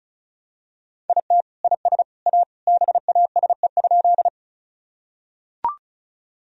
Morse code, heard as a keyed tone, about 34 wpm, 700 Hz. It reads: IT IS A BASE?
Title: Sentences from Top 1000 Words - Review Part 1 - 35wpm